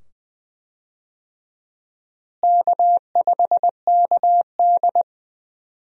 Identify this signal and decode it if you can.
Morse code, heard as a keyed tone, about 20 wpm, 700 Hz. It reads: K5KD
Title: USA Call Signs - 4 Characters - Repeated 2x - 20wpm